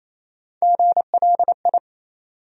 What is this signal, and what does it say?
Morse code, heard as a keyed tone, about 28 wpm, 700 Hz. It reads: GLI
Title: USA Call Sign Postfixes - 3 Letters 28wpm